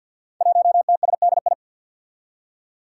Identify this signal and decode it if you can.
Morse code, heard as a keyed tone, about 47 wpm, 700 Hz. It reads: 1TSDI
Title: Five Challenging Letter-Number Combinations 50wpm